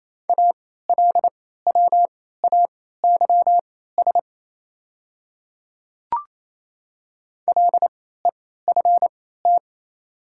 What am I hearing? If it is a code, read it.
Morse code, heard as a keyed tone, about 28 wpm, spaced as for 16 wpm, 700 Hz. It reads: ALWAYS LEFT
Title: 3x Character Spacing - Top 300 Words - 28wpm